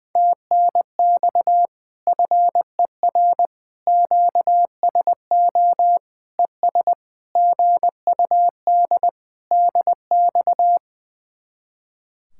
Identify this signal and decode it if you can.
Morse code, heard as a keyed tone, about 20 wpm, 700 Hz. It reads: TNX FER QSO ES GUD DX
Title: Speed-Racing Common QSO Phrases - 20wpm